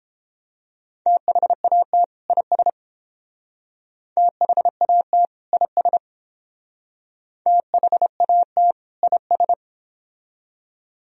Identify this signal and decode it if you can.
Morse code, heard as a keyed tone, about 30 wpm, 700 Hz. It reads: THAT IS THAT IS THAT IS